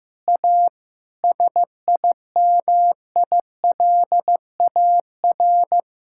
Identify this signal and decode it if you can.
Morse code, heard as a keyed tone, about 15 wpm, 700 Hz. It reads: A SIMILAR
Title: Sentences from Top 1000 Words - Review Part 5 - 15wpm